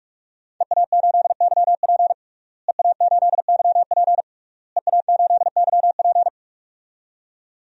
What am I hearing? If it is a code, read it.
Morse code, heard as a keyed tone, about 47 wpm, 700 Hz. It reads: EA8YP EA8YP EA8YP
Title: DXCC-100 Call Signs - 5 Characters - Repeated 3x - 45wpm